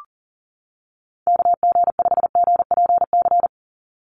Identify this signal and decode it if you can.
Morse code, heard as a keyed tone, about 40 wpm, 700 Hz. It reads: KG5GPC
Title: USA Call Signs 40wpm